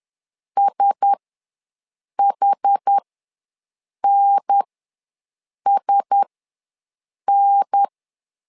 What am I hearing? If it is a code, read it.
Morse code, heard as a keyed tone, about 10 wpm, spaced as for 6 wpm, 800 Hz. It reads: SHNSN
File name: LEZIONI / 4 Q1S ws 10wpm 2208